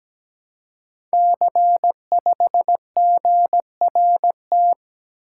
Morse code, heard as a keyed tone, about 17 wpm, 700 Hz. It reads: C5GRT
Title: Code Groups - 5 Letter:Numbers - 17wpm